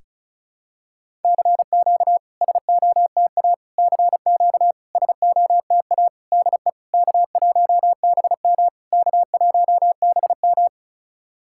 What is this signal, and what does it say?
Morse code, heard as a keyed tone, about 34 wpm, 690 Hz. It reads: CQ SOTA CQ SOTA DE K1BM K1BM
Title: SOTA QSO Practice 35wpm